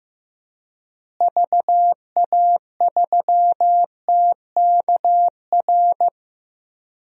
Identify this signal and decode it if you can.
Morse code, heard as a keyed tone, about 16 wpm, 700 Hz. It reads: VA3TKR
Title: DXCC-100 Call Signs - 6 Characters - Repeated 2x - 15wpm